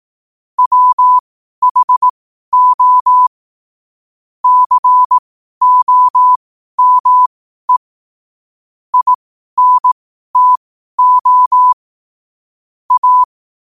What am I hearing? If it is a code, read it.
Morse code, heard as a keyed tone, about 18 wpm, spaced as for 12 wpm, 1000 Hz. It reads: WHO COME INTO A
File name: 12wpm